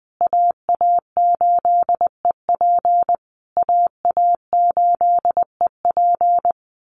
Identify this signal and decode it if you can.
Morse code, heard as a keyed tone, about 20 wpm, 700 Hz. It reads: AA8EP AA8EP
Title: USA Call Signs Repeated 3x 20wpm